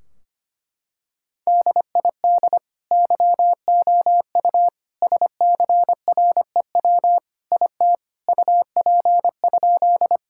Morse code, heard as a keyed tone, about 25 wpm, 700 Hz. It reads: DID YOU SCREW IT UP?
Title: Sentences from Top 1600 Words 25wpm